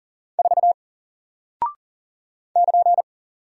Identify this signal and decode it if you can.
Morse code, heard as a keyed tone, about 40 wpm, 700 Hz. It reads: 4 <KN>
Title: Single Character (Letter:Number:Punctuation:Prosign) 40wpm